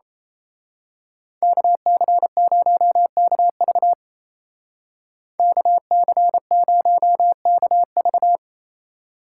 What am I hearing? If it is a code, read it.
Morse code, heard as a keyed tone, about 30 wpm, 700 Hz. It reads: KC0KV KC0KV
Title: Speed Racing - USA 2x2 Call Signs 28wpm